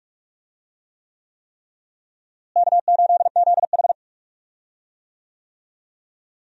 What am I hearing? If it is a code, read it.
Morse code, heard as a keyed tone, about 46 wpm, 700 Hz. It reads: K8ZH
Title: USA Call Signs - 4 Characters - Repeated 3x - 45wpm